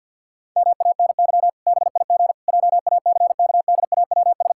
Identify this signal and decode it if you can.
Morse code, heard as a keyed tone, about 52 wpm, 700 Hz. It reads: MANY BIG JACKDAWS
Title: Pangram Challenge v2 - 50wpm